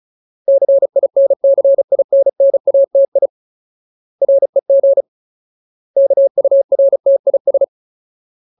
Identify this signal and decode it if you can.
Morse code, heard as a keyed tone, about 34 wpm, 550 Hz. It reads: CINCINNATI REG KURTIS
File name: Ultimate Word List / 35WPM Ultimate